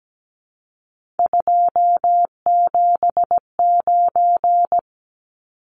Morse code, heard as a keyed tone, about 17 wpm, 700 Hz. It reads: 279
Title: Numbers - 3-Digit 17wpm